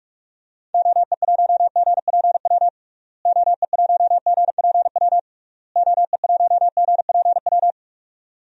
Morse code, heard as a keyed tone, about 46 wpm, 700 Hz. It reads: OE1GPW OE1GPW OE1GPW
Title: DXCC-100 Call Signs - 6 Characters - Repeated 3x - 45wpm